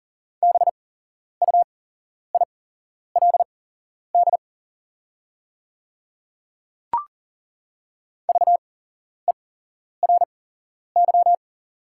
Morse code, heard as a keyed tone, about 40 wpm, spaced as for 11 wpm, 700 Hz. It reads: BUILD VERY
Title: 8x Character Spacing - Top 200 Words - 40wpm